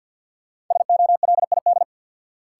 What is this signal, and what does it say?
Morse code, heard as a keyed tone, about 53 wpm, 700 Hz. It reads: SOLID
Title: Sentences from Top 1500 Words - New Word Drill Alt - 50wpm